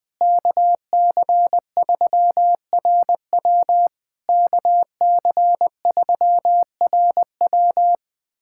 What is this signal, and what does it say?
Morse code, heard as a keyed tone, about 20 wpm, 700 Hz. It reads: KC3RW KC3RW
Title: POTA QSO Practice - 20wpm